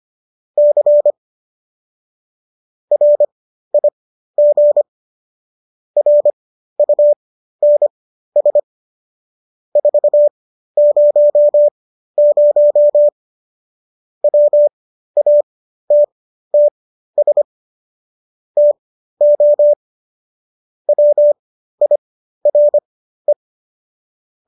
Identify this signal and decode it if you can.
Morse code, heard as a keyed tone, about 25 wpm, spaced as for 13 wpm, 590 Hz. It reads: C RIG RUNS 400 WATTS TO WIRE